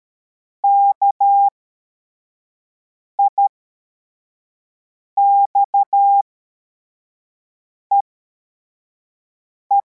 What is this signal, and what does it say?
Morse code, heard as a keyed tone, about 13 wpm, spaced as for 4 wpm, 800 Hz. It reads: KIXEE